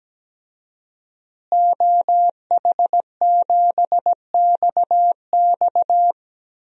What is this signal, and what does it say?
Morse code, heard as a keyed tone, about 17 wpm, 700 Hz. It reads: OH7XX
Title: DXCC-100 Call Signs - 5 Characters - Repeated 2x - 17wpm